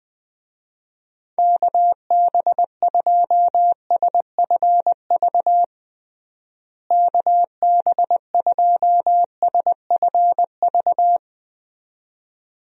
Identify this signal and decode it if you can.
Morse code, heard as a keyed tone, about 20 wpm, 700 Hz. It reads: KB2SFV KB2SFV
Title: USA Call Signs - 6 Characters - Repeated 2x - 20wpm